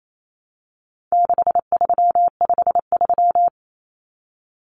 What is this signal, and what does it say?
Morse code, heard as a keyed tone, about 28 wpm, 700 Hz. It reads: 6353
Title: Numbers - 4-Digit 28wpm